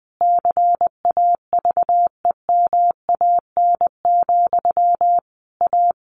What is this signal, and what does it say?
Morse code, heard as a keyed tone, about 20 wpm, 700 Hz. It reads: CAVEMAN, A
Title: A Sound Of Thunder 20wpm